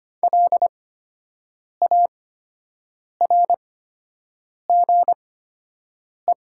Morse code, heard as a keyed tone, about 25 wpm, spaced as for 7 wpm, 700 Hz. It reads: LARGE
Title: 8x Character Spacing - Top 200 Words - 25wpm